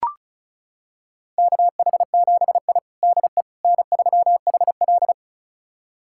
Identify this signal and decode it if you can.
Morse code, heard as a keyed tone, about 36 wpm, 700 Hz. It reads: KH7I DE N3HL
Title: USA Callsign DE Callsign - 4 Characters - 35wpm